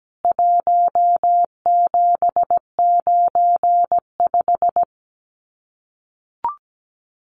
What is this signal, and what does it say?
Morse code, heard as a keyed tone, about 17 wpm, 700 Hz. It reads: 1795
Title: CWT Exchange 17wpm